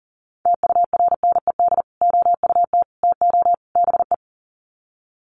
Morse code, heard as a keyed tone, about 40 wpm, 700 Hz. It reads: TURNED OUT TO BE
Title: Sets of 4 Words 40wpm